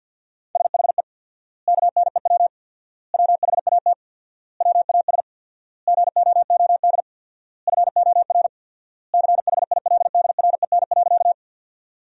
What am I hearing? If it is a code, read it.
Morse code, heard as a keyed tone, about 53 wpm, 700 Hz. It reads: SHE KNEW WHAT WAS GOOD FOR CHILDREN.
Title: 4x Word-Spacing - Sentences from Top 400 Words - 50wpm